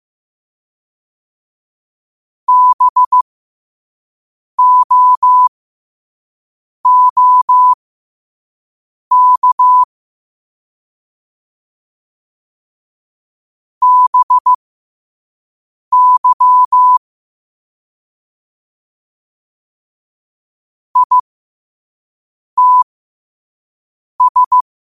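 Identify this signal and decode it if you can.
Morse code, heard as a keyed tone, about 15 wpm, spaced as for 5 wpm, 1000 Hz. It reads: BOOK BY ITS
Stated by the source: Morse code quote-of-the-day practice at 5 wpm.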